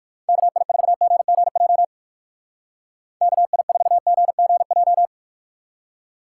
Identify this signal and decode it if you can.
Morse code, heard as a keyed tone, about 56 wpm, 700 Hz. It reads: KI4GGJ KI4GGJ
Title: Speed Racing - USA Call Signs 45wpm